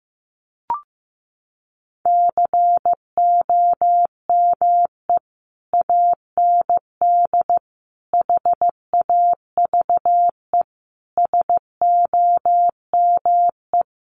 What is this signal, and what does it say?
Morse code, heard as a keyed tone, about 15 wpm, 700 Hz. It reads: COME AND HAVE SOME
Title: Sentences from Top 1500 Words 15wpm